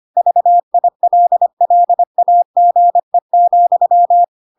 Morse code, heard as a keyed tone, about 25 wpm, 700 Hz. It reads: VILLAGE,